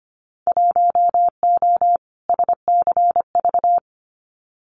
Morse code, heard as a keyed tone, about 25 wpm, 700 Hz. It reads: 1O SCV
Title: Winter Field Day Exchanges 25wpm